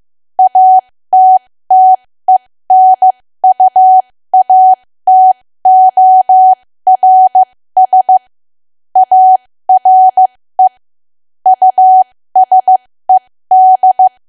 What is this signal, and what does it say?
Morse code, heard as a keyed tone, about 15 wpm, spaced as for 13 wpm, 750 Hz. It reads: ATTENUATORS ARE USED
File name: cwa / W1AW-Level2 / 13WPM